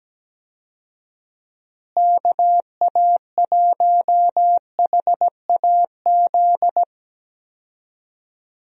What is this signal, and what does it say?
Morse code, heard as a keyed tone, about 17 wpm, 700 Hz. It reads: KA1HAZ
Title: USA Call Signs - 6 Characters - Repeated 2x - 17wpm